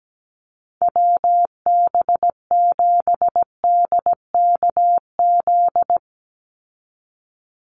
Morse code, heard as a keyed tone, about 17 wpm, 700 Hz. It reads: WB7DKZ
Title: Call Signs Repeated 3x - 17wpm